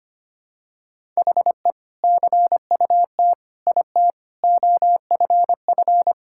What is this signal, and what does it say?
Morse code, heard as a keyed tone, about 25 wpm, 700 Hz. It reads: HE CUT IT OFF
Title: Sentences from Top 1600 Words 25wpm